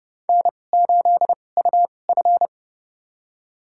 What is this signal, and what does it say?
Morse code, heard as a keyed tone, about 30 wpm, spaced as for 22 wpm, 700 Hz. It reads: N8UF
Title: USA Call Signs - 4 Characters - 2x Character Spacing - 30wpm